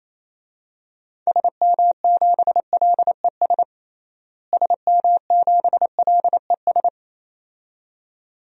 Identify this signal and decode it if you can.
Morse code, heard as a keyed tone, about 28 wpm, 700 Hz. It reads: SM7LES SM7LES
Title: DXCC-100 Call Signs - 6 Characters - Repeated 2x - 28wpm